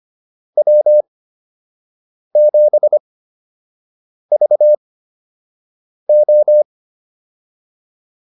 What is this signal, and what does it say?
Morse code, heard as a keyed tone, about 25 wpm, spaced as for 6 wpm, 600 Hz. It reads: W7VO
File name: QSO 105 06